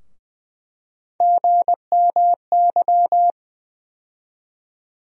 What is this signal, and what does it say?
Morse code, heard as a keyed tone, about 20 wpm, 700 Hz. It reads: GMY
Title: USA Call Sign Postfixes - 3 Letters 20wpm